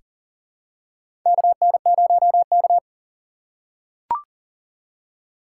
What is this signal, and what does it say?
Morse code, heard as a keyed tone, about 40 wpm, 700 Hz. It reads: KN0K
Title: USA Call Signs - 4x Character Spacing - 40wpm